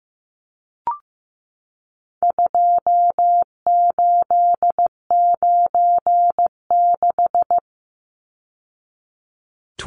Morse code, heard as a keyed tone, about 15 wpm, 700 Hz. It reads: 2896
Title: Numbers - 4-Digit 15wpm